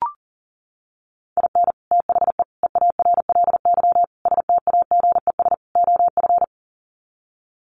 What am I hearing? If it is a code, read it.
Morse code, heard as a keyed tone, about 40 wpm, 700 Hz. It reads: IN THE EARLY STAGES OF
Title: Sets of 5 Words Encore 40wpm